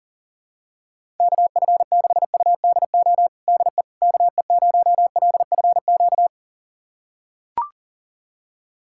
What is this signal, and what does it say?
Morse code, heard as a keyed tone, about 40 wpm, 700 Hz. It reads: KF6UDO DE KE0LFQ
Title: USA Callsign DE Callsign 40wpm